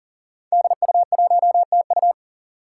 Morse code, heard as a keyed tone, about 40 wpm, 700 Hz. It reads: DU1TU
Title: Five Challenging Letter-Number Combinations 40wpm